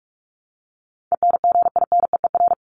Morse code, heard as a keyed tone, about 44 wpm, 700 Hz. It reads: ENGINEER
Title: ING Suffix - Top 100 Words with Base Words 45wpm